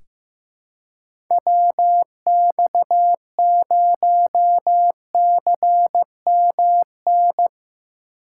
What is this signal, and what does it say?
Morse code, heard as a keyed tone, about 15 wpm, 700 Hz. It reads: WX0CMN
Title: USA Call Signs - 2x Character Spacing - 15wpm